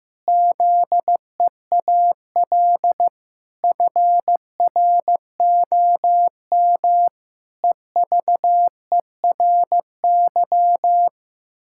Morse code, heard as a keyed tone, about 16 wpm, 700 Hz. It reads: ZEAL FROM EVERY
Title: Pangram Challenge v2 - 15wpm